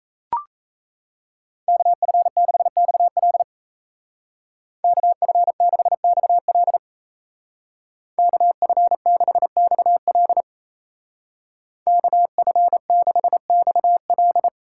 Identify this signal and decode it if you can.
Morse code, heard as a keyed tone, about 36 wpm, 700 Hz. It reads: KF6XL KF6XL KF6XL KF6XL